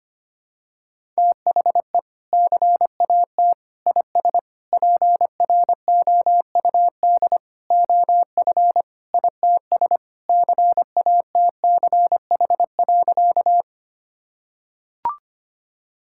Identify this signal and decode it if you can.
Morse code, heard as a keyed tone, about 25 wpm, 690 Hz. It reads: THE CAT IS PROUD OF ITS CATCH.
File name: practice / 25wpm / Sentences with 2-Word Phrases - 25wpm